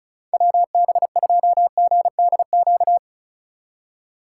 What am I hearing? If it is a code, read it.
Morse code, heard as a keyed tone, about 36 wpm, 700 Hz. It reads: WB2GDQ
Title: USA Call Signs - 8x Character Spacing - 35wpm